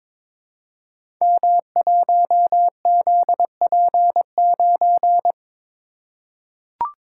Morse code, heard as a keyed tone, about 22 wpm, 700 Hz. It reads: M1ZP9
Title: Code Groups - 5 Letter:Numbers - 22wpm